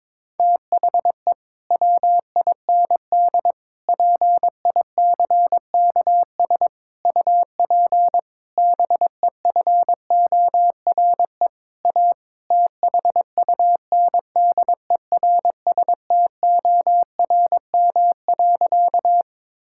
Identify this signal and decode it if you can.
Morse code, heard as a keyed tone, about 22 wpm, 700 Hz. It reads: THE WIND PICKS UP BEFORE A THUNDERSTORM.
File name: Difficult Sentences with Hints - 22wpm